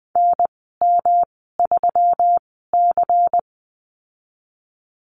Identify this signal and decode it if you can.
Morse code, heard as a keyed tone, about 20 wpm, spaced as for 15 wpm, 700 Hz. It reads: NM3C
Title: 154 Lesson 25g - Letter p - Call Signs 20wpm